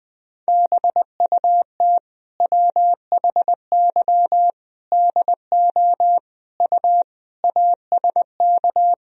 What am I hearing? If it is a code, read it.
Morse code, heard as a keyed tone, about 20 wpm, 700 Hz. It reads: BUT WHY DO U ASK
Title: Sentences from Top 200 Words and Abbreviations 20wpm